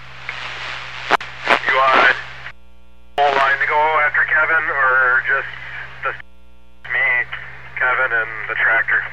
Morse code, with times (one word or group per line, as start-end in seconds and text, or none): none